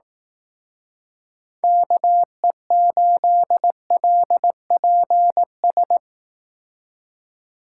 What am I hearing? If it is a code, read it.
Morse code, heard as a keyed tone, about 18 wpm, 700 Hz. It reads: KE8LPS
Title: Speed Racing - USA Call Signs 15wpm